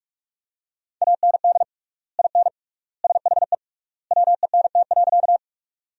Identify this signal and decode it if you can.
Morse code, heard as a keyed tone, about 42 wpm, 700 Hz. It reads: AND IN SHE WENT.
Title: 3x Word-Spacing - Sentences from Top 200 Words - 45wpm